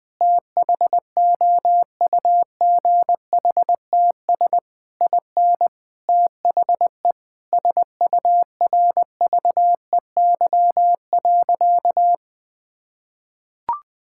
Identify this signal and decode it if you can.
Morse code, heard as a keyed tone, about 21 wpm, 700 Hz. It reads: THOUGHTS IN THE SURVEY.
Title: Sentences with 7-Word Phrases - 20wpm